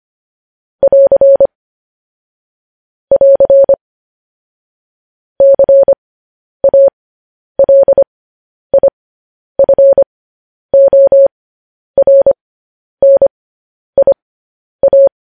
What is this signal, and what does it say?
Morse code, heard as a keyed tone, about 25 wpm, spaced as for 10 wpm, 550 Hz. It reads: <AR> <AR> CALIFORNIA